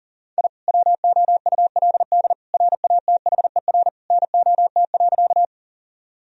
Morse code, heard as a keyed tone, about 40 wpm, 700 Hz. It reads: I WOULD RATHER NOT.
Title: Sentences from Top 1000 Words - Review Part 4 - 40wpm